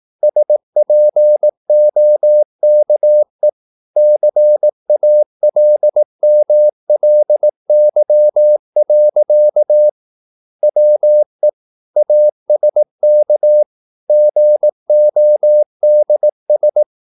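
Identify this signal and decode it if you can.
Morse code, heard as a keyed tone, about 18 wpm, 600 Hz. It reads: SPOKE CALMLY. WE ASK GODS
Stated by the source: Morse practice at 18 wpm, English text, from CW med Gnister.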